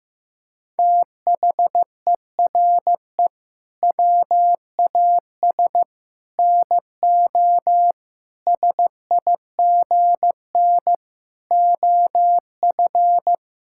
Morse code, heard as a keyed tone, about 15 wpm, 700 Hz. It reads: THERE WAS NO SIGN OF
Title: Sentences from Top 1000 Words - Review Part 6 - 15wpm